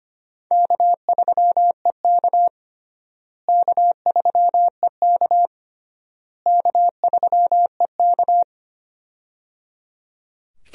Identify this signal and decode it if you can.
Morse code, heard as a keyed tone, about 25 wpm, 700 Hz. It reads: K3EK K3EK K3EK
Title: USA Call Signs - 4 Characters - Repeated 3x - 25wpm